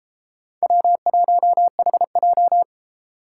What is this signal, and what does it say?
Morse code, heard as a keyed tone, about 33 wpm, 700 Hz. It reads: W1HJ